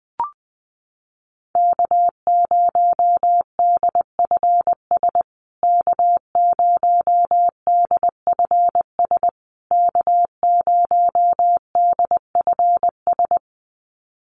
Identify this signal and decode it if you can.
Morse code, heard as a keyed tone, about 20 wpm, 700 Hz. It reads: K0DFS K0DFS K0DFS
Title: USA Call Signs Repeated 3x 20wpm